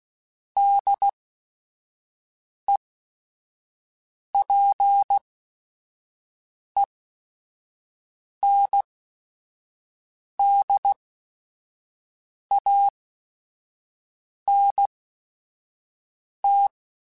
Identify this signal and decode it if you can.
Morse code, heard as a keyed tone, about 16 wpm, spaced as for 5 wpm, 800 Hz. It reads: DEPENDANT
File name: Technical Subjects / Technical Subjects-mp3s / XL @5